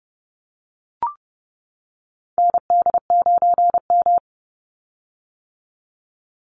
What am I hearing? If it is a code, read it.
Morse code, heard as a keyed tone, about 30 wpm, 700 Hz. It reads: ND9M
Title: Call Signs - 30wpm